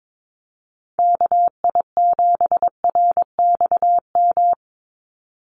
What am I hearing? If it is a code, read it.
Morse code, heard as a keyed tone, about 22 wpm, 700 Hz. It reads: KI7RXM
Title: USA Call Signs 22wpm